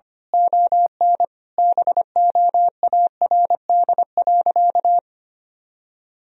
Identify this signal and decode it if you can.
Morse code, heard as a keyed tone, about 25 wpm, 700 Hz. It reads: ON BOARD.
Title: Sentences from Top 1000 Words - Review Part 4 - 25wpm